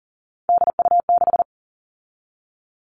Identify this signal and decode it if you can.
Morse code, heard as a keyed tone, about 40 wpm, 700 Hz. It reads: DU6
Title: All Cut Numbers - 3-Digit 40wpm